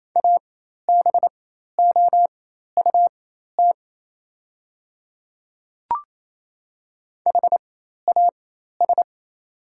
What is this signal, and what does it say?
Morse code, heard as a keyed tone, about 28 wpm, spaced as for 13 wpm, 690 Hz. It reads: ABOUT HAS
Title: 4x Character Spacing - Top 100 Words - 28wpm